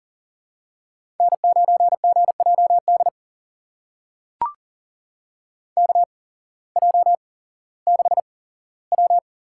Morse code, heard as a keyed tone, about 40 wpm, spaced as for 19 wpm, 700 Hz. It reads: N9GJD KJ6W
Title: USA Call Signs - 8x Character Spacing - 40wpm